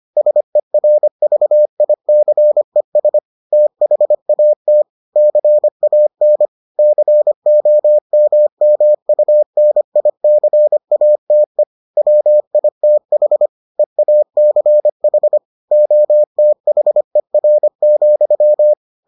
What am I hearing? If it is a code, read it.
Morse code, heard as a keyed tone, about 25 wpm, 600 Hz. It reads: SERVICES THAT CAN COMMUNICATE WITH EACH OTHER,